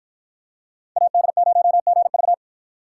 Morse code, heard as a keyed tone, about 48 wpm, 700 Hz. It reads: AD0GV